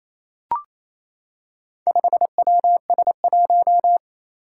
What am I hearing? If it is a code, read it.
Morse code, heard as a keyed tone, about 27 wpm, 700 Hz. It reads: 5WS1
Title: Four Challenging Letter-Number Combinations 28wpm